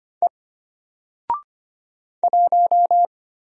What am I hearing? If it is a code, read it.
Morse code, heard as a keyed tone, about 25 wpm, 700 Hz. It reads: E 1
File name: Single-Letter-Number-25wpm